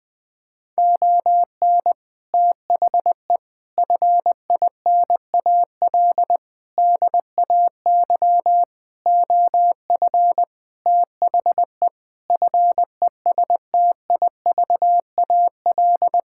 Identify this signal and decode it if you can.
Morse code, heard as a keyed tone, about 20 wpm, 700 Hz. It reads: ON THE FINAL DAY OF THE FESTIVAL
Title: Sentences with 6-Word Phrases - Spoken Beginning - 20wpm